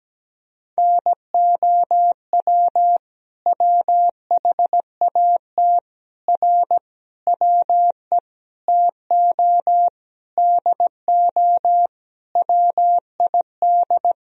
Morse code, heard as a keyed tone, about 17 wpm, 700 Hz. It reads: NOW WHAT R WE TO DO WID
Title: Sentences from Top 100 Words and Abbreviations 17wpm